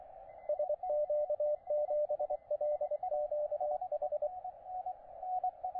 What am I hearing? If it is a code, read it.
Morse code, heard as a keyed tone, about 23 wpm, 600 Hz, with radio noise. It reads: SQ7LQH
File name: OX3LX, NA-243 30CW